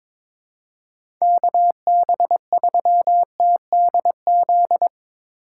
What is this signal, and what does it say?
Morse code, heard as a keyed tone, about 22 wpm, 700 Hz. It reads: KB3TDZ
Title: USA Call Signs - 6 Characters - 4x Character Spacing - 22wpm